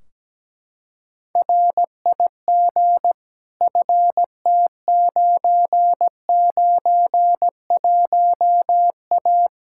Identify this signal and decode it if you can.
Morse code, heard as a keyed tone, about 17 wpm, 700 Hz. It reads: RIG FT991A
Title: Practice QSO - Exchange 2 - Info - 17wpm